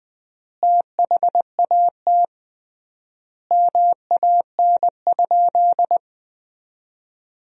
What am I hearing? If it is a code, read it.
Morse code, heard as a keyed tone, about 20 wpm, 700 Hz. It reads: THAT MAN?
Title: 3x Word-Spacing - Sentences from Top 200 Words - 20wpm